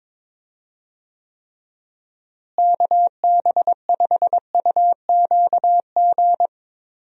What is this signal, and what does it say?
Morse code, heard as a keyed tone, about 22 wpm, 700 Hz. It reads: KB5UQG
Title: USA Call Signs - 6 Characters - Repeated 2x - 22wpm